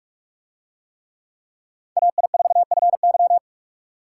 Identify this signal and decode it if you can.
Morse code, heard as a keyed tone, about 45 wpm, 700 Hz. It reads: AI4FY